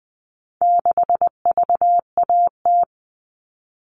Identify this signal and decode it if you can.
Morse code, heard as a keyed tone, about 20 wpm, 700 Hz. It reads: 6VAT
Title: All Cut Numbers - 4-Digit 20wpm